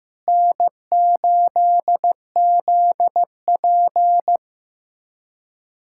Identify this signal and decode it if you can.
Morse code, heard as a keyed tone, about 15 wpm, 700 Hz. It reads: N8ZP
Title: USA Callsign DE Callsign - 4 Characters - 15wpm